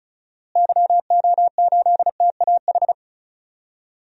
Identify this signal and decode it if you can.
Morse code, heard as a keyed tone, about 36 wpm, 700 Hz. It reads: YO8TAH
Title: DXCC-100 Call Signs - 6 Characters - Repeated 2x - 35wpm